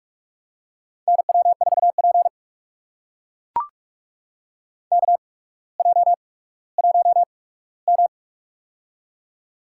Morse code, heard as a keyed tone, about 44 wpm, spaced as for 19 wpm, 700 Hz. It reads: NW4P KJ1M
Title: USA Call Signs - 4 Characters - 8x Character Spacing - 45wpm